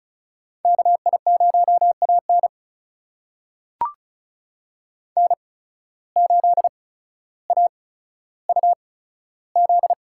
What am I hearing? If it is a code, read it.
Morse code, heard as a keyed tone, about 36 wpm, spaced as for 15 wpm, 700 Hz. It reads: KI0AN N8AUZ
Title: USA Call Signs - 8x Character Spacing - 35wpm